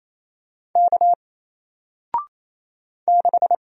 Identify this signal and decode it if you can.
Morse code, heard as a keyed tone, about 27 wpm, 700 Hz. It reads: K 6
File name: Single-Letter-Number-28wpm